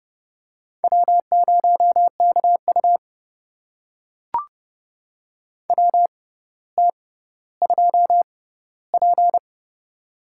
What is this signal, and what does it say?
Morse code, heard as a keyed tone, about 32 wpm, spaced as for 16 wpm, 700 Hz. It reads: W0KU WT2P
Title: USA Call Signs - 4 Characters - 6x Character Spacing - 30wpm